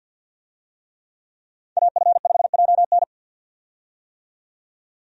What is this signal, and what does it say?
Morse code, heard as a keyed tone, about 47 wpm, 700 Hz. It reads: AF5JN
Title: USA Call Signs - 5 Characters - 50wpm